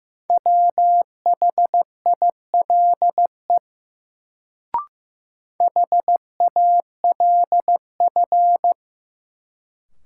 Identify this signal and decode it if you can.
Morse code, heard as a keyed tone, about 15 wpm, 700 Hz. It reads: WHILE HALF
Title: Sets of 3 Words - New Word Drill - 15wpm